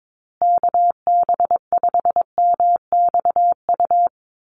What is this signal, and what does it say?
Morse code, heard as a keyed tone, about 22 wpm, 700 Hz. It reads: KB5MXU
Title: USA Call Signs Repeated 2x 22wpm